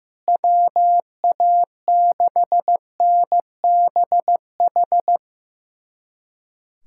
Morse code, heard as a keyed tone, about 16 wpm, 700 Hz. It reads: WA6NBH
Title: USA Callsign DE Callsign - 6 Characters - 15wpm